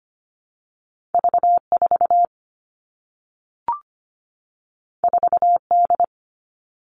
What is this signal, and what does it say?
Morse code, heard as a keyed tone, about 25 wpm, 700 Hz. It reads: V4 4D
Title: All Cut Numbers - 2-Digit 25wpm